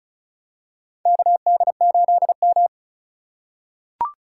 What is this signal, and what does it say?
Morse code, heard as a keyed tone, about 34 wpm, 700 Hz. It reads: KD8M
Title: USA Call Signs - 4x Character Spacing - 35wpm